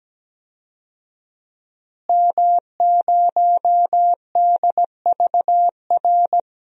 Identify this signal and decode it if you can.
Morse code, heard as a keyed tone, about 17 wpm, 700 Hz. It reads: M0DVR
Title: DXCC-100 Call Signs - 5 Characters - Repeated 2x - 17wpm